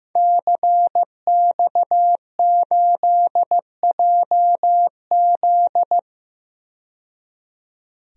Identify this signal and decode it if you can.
Morse code, heard as a keyed tone, about 15 wpm, 700 Hz. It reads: CX8JZ
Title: DXCC-100 Call Signs - 5 Characters - 15wpm